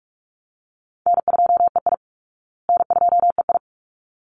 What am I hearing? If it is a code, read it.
Morse code, heard as a keyed tone, about 44 wpm, 700 Hz. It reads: N2EI N2EI
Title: Easy Call Signs Repeated 2x - 45wpm